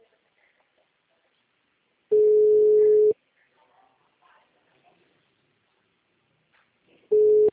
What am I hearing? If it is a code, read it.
Ringback tone (Europe, 425 Hz).
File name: out- -2017- - - .42562